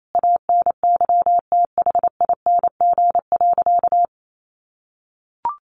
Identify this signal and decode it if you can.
Morse code, heard as a keyed tone, about 28 wpm, 700 Hz. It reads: ANYTHING.
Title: Sentences from Top 1000 Words 28wpm